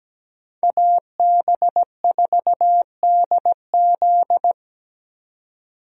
Morse code, heard as a keyed tone, about 17 wpm, 700 Hz. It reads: AB4DZ
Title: USA Call Signs - 5 Characters - 17wpm